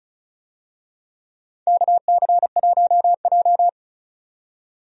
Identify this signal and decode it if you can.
Morse code, heard as a keyed tone, about 36 wpm, 700 Hz. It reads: KC1J
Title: USA Call Signs - 4 Characters - 35wpm